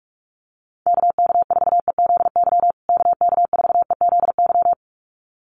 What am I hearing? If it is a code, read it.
Morse code, heard as a keyed tone, about 44 wpm, 700 Hz. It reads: KK4EZY KK4EZY
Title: USA Call Signs Repeated 2x 45wpm